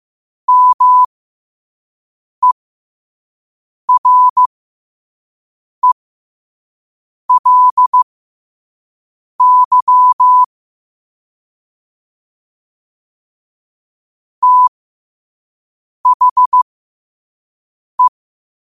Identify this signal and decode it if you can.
Morse code, heard as a keyed tone, about 15 wpm, spaced as for 5 wpm, 1000 Hz. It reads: MERELY THE
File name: qotdfiles / 5wpm